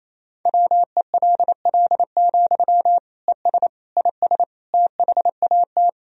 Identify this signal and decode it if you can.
Morse code, heard as a keyed tone, about 28 wpm, 700 Hz. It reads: WELL, ES IS THAT
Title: Sentences from Top 200 Words and Abbreviations 28wpm